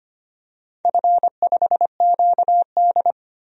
Morse code, heard as a keyed tone, about 25 wpm, 690 Hz. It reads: F5QD
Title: DXCC-100 Call Signs - 4 Characters - 25wpm